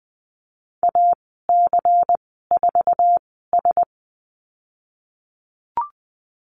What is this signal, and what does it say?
Morse code, heard as a keyed tone, about 19 wpm, spaced as for 14 wpm, 700 Hz. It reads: AC4S
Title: 147 Lesson 24g - Letter y - Call Signs 20wpm